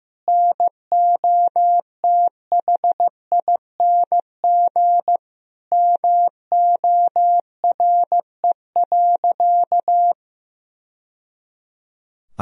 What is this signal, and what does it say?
Morse code, heard as a keyed tone, about 15 wpm, 700 Hz. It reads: NOTHING MORE.
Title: Sentences from Top 1000 Words - Review Part 1 - 15wpm